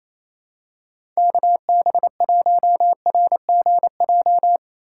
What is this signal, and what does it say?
Morse code, heard as a keyed tone, about 28 wpm, 700 Hz. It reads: KB1RGJ
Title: USA Call Signs - 3x Character Spacing - 28wpm